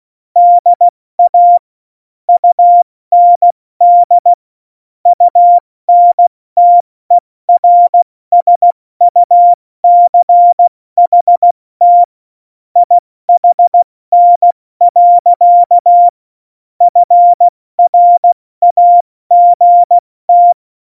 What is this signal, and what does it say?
Morse code, heard as a keyed tone, about 16 wpm, 700 Hz. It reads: DA UND UNTERSUCHT IHN. FRAGT
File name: Morsekurs / Witze / Witz 30. Januar 2018, 70 F80